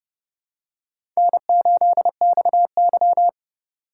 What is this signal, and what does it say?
Morse code, heard as a keyed tone, about 32 wpm, 700 Hz. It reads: N8XY
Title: USA Call Signs - 4 Characters - 6x Character Spacing - 30wpm